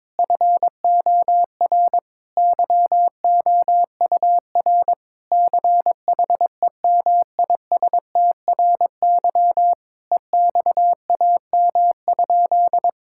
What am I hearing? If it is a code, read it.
Morse code, heard as a keyed tone, about 22 wpm, 700 Hz. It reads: FOR YOUR CHEMISTRY EXAM?
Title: Sentences from Top 1000 Words - Review Part 6 - 22wpm